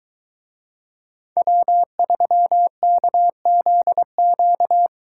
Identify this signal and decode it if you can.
Morse code, heard as a keyed tone, about 23 wpm, 700 Hz. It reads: W3KZQ